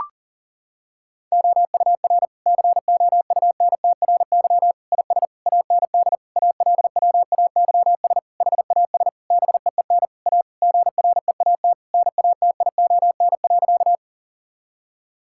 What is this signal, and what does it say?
Morse code, heard as a keyed tone, about 40 wpm, 700 Hz. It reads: OUR COUNTRY IS AND ALWAYS HAS BEEN A GREAT NATION.
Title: Sentences from Top 1000 Words - Review Part 6 - 40wpm